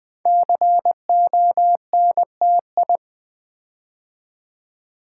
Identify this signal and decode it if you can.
Morse code, heard as a keyed tone, about 20 wpm, 700 Hz. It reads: CONTI
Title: Build A Word - 5 Letters At A Time - 10-Letter Words - 20wpm